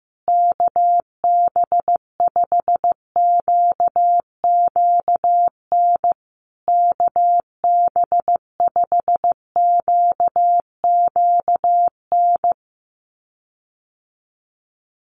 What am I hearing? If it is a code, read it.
Morse code, heard as a keyed tone, about 15 wpm, 700 Hz. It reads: KB5QQN KB5QQN
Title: USA Call Signs Repeated 3x 15wpm